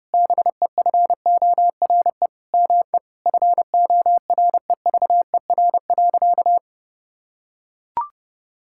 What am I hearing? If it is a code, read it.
Morse code, heard as a keyed tone, about 32 wpm, 700 Hz. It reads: BEFORE ME FOREVER.
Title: Sentences from Top 1600 Words 30wpm